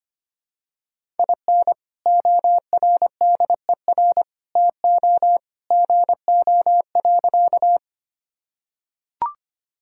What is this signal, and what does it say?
Morse code, heard as a keyed tone, about 25 wpm, 700 Hz. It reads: IN ORDER TO GO.
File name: practice / 25wpm / Sentences with 2-Word Phrases - 25wpm